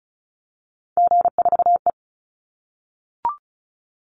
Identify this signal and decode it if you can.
Morse code, heard as a keyed tone, about 34 wpm, 700 Hz. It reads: G4E
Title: Call Signs - 35wpm